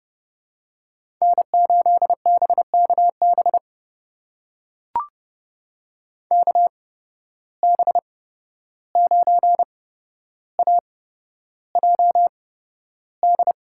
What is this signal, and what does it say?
Morse code, heard as a keyed tone, about 30 wpm, spaced as for 12 wpm, 700 Hz. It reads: N8BKB KB9AJD
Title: USA Call Signs - 8x Character Spacing - 30wpm